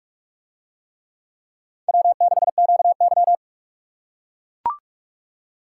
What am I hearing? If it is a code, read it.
Morse code, heard as a keyed tone, about 40 wpm, 700 Hz. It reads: W6QY